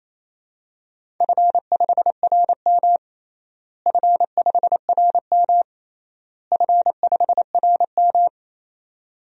Morse code, heard as a keyed tone, about 28 wpm, 700 Hz. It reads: F5RM F5RM F5RM
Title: DXCC-100 Call Signs - 4 Characters - Repeated 3x - 28wpm